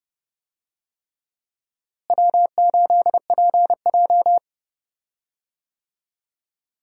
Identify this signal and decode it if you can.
Morse code, heard as a keyed tone, about 31 wpm, 700 Hz. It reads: W8PJ
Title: USA Call Signs - 4 Characters - Repeated 3x - 30wpm